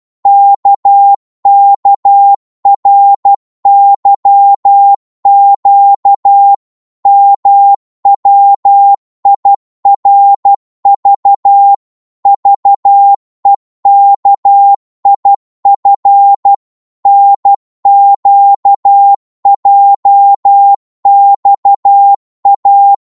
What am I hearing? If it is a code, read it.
Morse code, heard as a keyed tone, about 12 wpm, 800 Hz. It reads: KKRYQ MWIRV VEKIF NQJXA
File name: Morse / ltr-12wpm